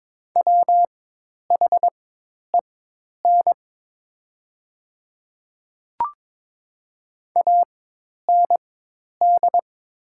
Morse code, heard as a keyed tone, about 22 wpm, spaced as for 10 wpm, 700 Hz. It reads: WHEN AND
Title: 4x Character Spacing - Top 100 Words - 22wpm